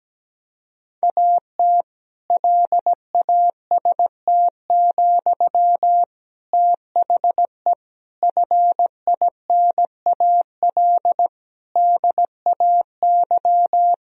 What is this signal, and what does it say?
Morse code, heard as a keyed tone, about 17 wpm, 700 Hz. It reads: AT LAST, THE FINAL DAY
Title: Sentences from Top 1000 Words - Review Part 2 - 17wpm